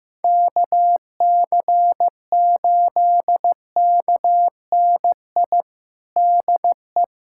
Morse code, heard as a keyed tone, about 15 wpm, 700 Hz. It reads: KC8KNI DE
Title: USA Callsign DE Callsign 15wpm